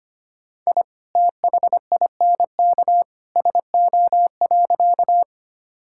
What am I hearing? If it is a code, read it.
Morse code, heard as a keyed tone, about 25 wpm, 700 Hz. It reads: I THINK SO.
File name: practice / 25wpm / Sentences with 2-Word Phrases - 25wpm